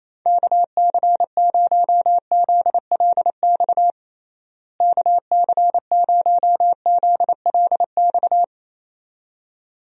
Morse code, heard as a keyed tone, about 28 wpm, 700 Hz. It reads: KC0ZLX KC0ZLX
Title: USA Call Signs - 6 Characters - Repeated 2x - 28wpm